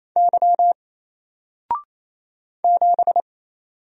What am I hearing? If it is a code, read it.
Morse code, heard as a keyed tone, about 28 wpm, 700 Hz. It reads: Y 7
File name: Single-Letter-Number-28wpm